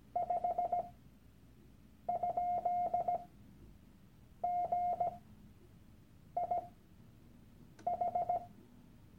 Morse code, heard as a keyed tone, about 17 wpm, spaced as for 6 wpm, 710 Hz. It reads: 5?GIH
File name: Workshop Dateien / CW-Online-Kurs / Lektion 001